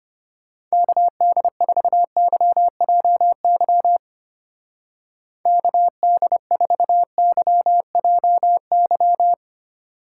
Morse code, heard as a keyed tone, about 31 wpm, 700 Hz. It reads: KD4YJY KD4YJY
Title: Speed Racing - USA Call Signs 25wpm